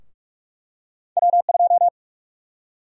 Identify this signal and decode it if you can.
Morse code, heard as a keyed tone, about 46 wpm, 700 Hz. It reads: W2
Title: USA Call Sign Prefixes - 45wpm